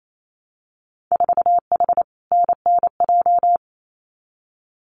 Morse code, heard as a keyed tone, about 28 wpm, 700 Hz. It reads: 4H NNJ
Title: Winter Field Day Exchanges Encore 28wpm